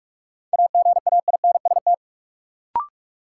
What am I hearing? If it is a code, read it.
Morse code, heard as a keyed tone, about 46 wpm, 700 Hz. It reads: AGAINST
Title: Sets of 2 Words - New Word Drill - 45wpm